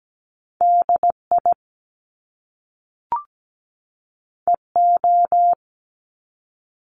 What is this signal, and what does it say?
Morse code, heard as a keyed tone, about 17 wpm, 700 Hz. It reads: DI EO
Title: Top Two Letter Combinations in Words 17wpm